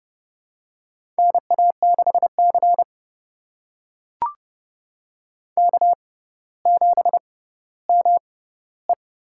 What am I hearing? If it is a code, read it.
Morse code, heard as a keyed tone, about 30 wpm, spaced as for 15 wpm, 700 Hz. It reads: NA6C K7ME
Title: USA Call Signs - 4 Characters - 6x Character Spacing - 30wpm